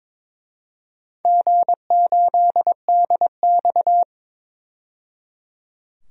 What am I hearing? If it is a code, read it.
Morse code, heard as a keyed tone, about 23 wpm, 700 Hz. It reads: G8DX
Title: DXCC-100 Call Signs - 4 Characters - 22wpm